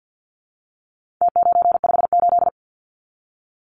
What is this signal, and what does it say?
Morse code, heard as a keyed tone, about 50 wpm, 700 Hz. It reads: T958
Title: Numbers - Contest Style - 4-Digit 50wpm